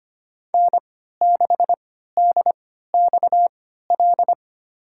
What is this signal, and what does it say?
Morse code, heard as a keyed tone, about 25 wpm, spaced as for 14 wpm, 700 Hz. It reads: N6DXL
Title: USA Call Signs - 5 Characters - 3x Character Spacing - 25wpm